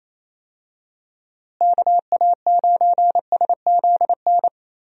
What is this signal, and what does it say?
Morse code, heard as a keyed tone, about 28 wpm, 700 Hz. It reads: KA9SZN
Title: USA Call Signs - 6 Characters - Repeated 2x - 28wpm